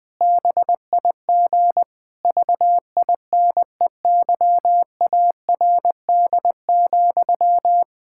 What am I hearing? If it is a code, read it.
Morse code, heard as a keyed tone, about 21 wpm, 700 Hz. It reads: BIG VINEYARD,
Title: Pangram Challenge v2 - 20wpm